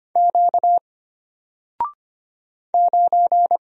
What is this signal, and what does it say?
Morse code, heard as a keyed tone, about 25 wpm, 700 Hz. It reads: Q 9
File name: Single-Letter-Number-25wpm